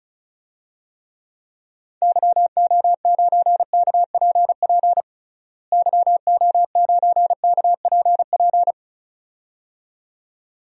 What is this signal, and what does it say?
Morse code, heard as a keyed tone, about 36 wpm, 700 Hz. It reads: YO9KPP YO9KPP
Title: DXCC-100 Call Signs - 6 Characters - Repeated 2x - 35wpm